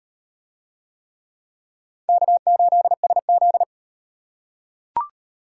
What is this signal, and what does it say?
Morse code, heard as a keyed tone, about 38 wpm, 700 Hz. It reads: K8SZ